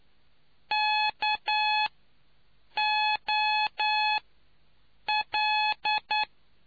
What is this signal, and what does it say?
Morse code, heard as a keyed tone, about 9 wpm, spaced as for 6 wpm, 850 Hz. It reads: KOL